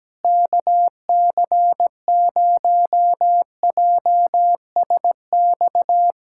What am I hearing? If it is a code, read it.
Morse code, heard as a keyed tone, about 17 wpm, 700 Hz. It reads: KC0JSX
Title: USA Callsign DE Callsign 17wpm